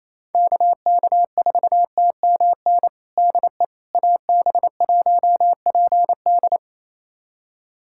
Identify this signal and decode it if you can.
Morse code, heard as a keyed tone, about 28 wpm, 700 Hz. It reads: KK4TMN DE AB1PD
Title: USA Callsign DE Callsign 28wpm